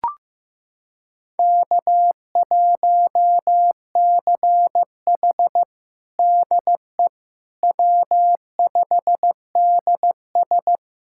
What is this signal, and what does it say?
Morse code, heard as a keyed tone, about 16 wpm, 700 Hz. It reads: K1CH DE W5DS
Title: USA Callsign DE Callsign - 4 Characters - 15wpm